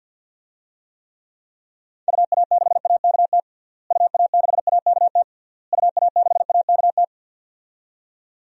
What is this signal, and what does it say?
Morse code, heard as a keyed tone, about 48 wpm, 700 Hz. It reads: UA6AKT UA6AKT UA6AKT
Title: DXCC-100 Call Signs - 6 Characters - Repeated 3x - 50wpm